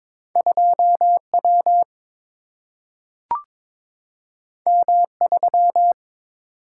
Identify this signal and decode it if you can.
Morse code, heard as a keyed tone, about 22 wpm, 700 Hz. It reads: 2W M3
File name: practice / 22wpm / Two Letter-Number Combinations - 22wpm